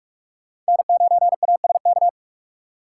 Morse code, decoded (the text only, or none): N9ASK